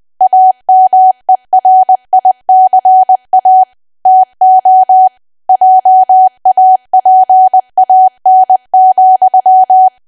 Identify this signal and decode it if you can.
Morse code, heard as a keyed tone, about 20 wpm, 750 Hz. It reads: AMERICA TO JAPAN,
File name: cwa / W1AW-Level3 / 20WPM